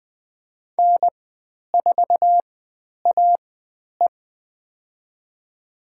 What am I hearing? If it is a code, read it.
Morse code, heard as a keyed tone, about 20 wpm, spaced as for 10 wpm, 700 Hz. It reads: N4AE
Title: 109 Lesson 18g - Letter m - Call Signs 20wpm